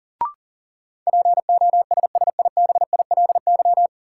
Morse code, heard as a keyed tone, about 40 wpm, 700 Hz. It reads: POSSIBILY
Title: Sentences from Top 1300 Words - New Word Drill - 40wpm